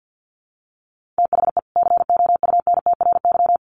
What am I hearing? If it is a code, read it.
Morse code, heard as a keyed tone, about 50 wpm, 700 Hz. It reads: THE COUNTRY
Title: Sets of 2 Words 50wpm